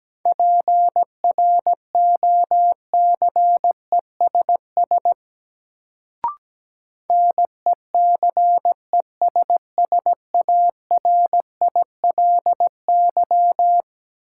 Morse code, heard as a keyed tone, about 17 wpm, 700 Hz. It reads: PROCESS NECESSARILY
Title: Sentences from Top 1400 Words - New Word Drill Alt - 17wpm